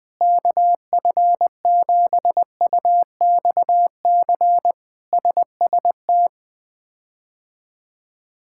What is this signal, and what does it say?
Morse code, heard as a keyed tone, about 20 wpm, 700 Hz. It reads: KF7UXC SST
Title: K1USN SST QSO Practice - 20wpm